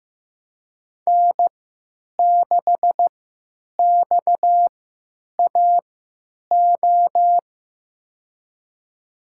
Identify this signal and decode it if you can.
Morse code, heard as a keyed tone, about 15 wpm, spaced as for 9 wpm, 700 Hz. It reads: N6XAO
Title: USA Call Signs - 3x Character Spacing - 15wpm